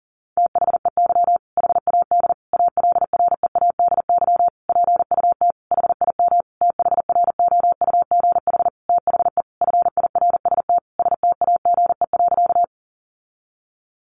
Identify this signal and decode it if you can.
Morse code, heard as a keyed tone, about 40 wpm, 700 Hz. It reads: THEY HAD ALREADY PUT HIM THROUGH THE FIRST STAGE.
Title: Sentences from Top 1200 Words 40wpm